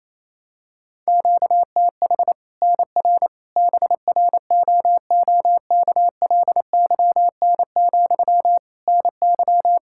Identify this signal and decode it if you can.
Morse code, heard as a keyed tone, about 28 wpm, 700 Hz. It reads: QTH NR BROOKLYN, NY
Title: Practice QSO - Exchange 1 - Beginning - 28wpm